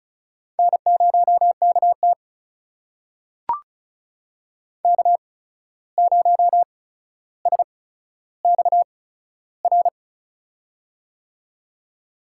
Morse code, heard as a keyed tone, about 36 wpm, spaced as for 14 wpm, 700 Hz. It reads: N0KT K0SXR
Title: USA Call Signs - 8x Character Spacing - 35wpm